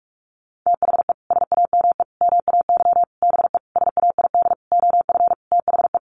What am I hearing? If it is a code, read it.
Morse code, heard as a keyed tone, about 44 wpm, 700 Hz. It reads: THE SAME MAY BE SAID OF THE
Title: Sentences from Top 1100 Words 45wpm